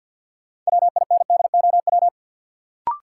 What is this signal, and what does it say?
Morse code, heard as a keyed tone, about 53 wpm, 700 Hz. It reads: WINDOW
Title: Sentences from Top 1600 Words - New Word Drill Alt - 50wpm